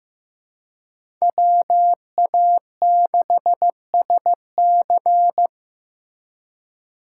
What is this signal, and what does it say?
Morse code, heard as a keyed tone, about 18 wpm, 700 Hz. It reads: WA6SC
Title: Speed Racing - USA 2x2 Call Signs 15wpm